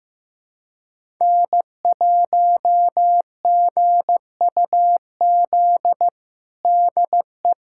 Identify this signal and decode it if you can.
Morse code, heard as a keyed tone, about 15 wpm, 700 Hz. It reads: N1GUZ DE
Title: USA Callsign DE Callsign 15wpm